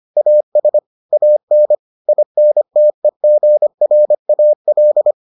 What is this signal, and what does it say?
Morse code, heard as a keyed tone, about 25 wpm, 600 Hz. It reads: AS AN INTEGRAL